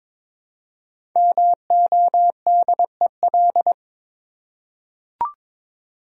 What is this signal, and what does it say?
Morse code, heard as a keyed tone, about 22 wpm, 700 Hz. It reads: MODEL
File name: 5 Letter Words - First 3 Letters Given - 22wpm